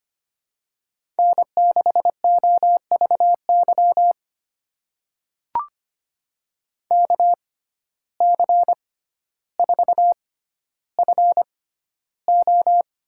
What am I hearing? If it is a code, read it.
Morse code, heard as a keyed tone, about 25 wpm, spaced as for 13 wpm, 700 Hz. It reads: N6OVY KC4FO
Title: USA Call Signs - 6x Character Spacing - 25wpm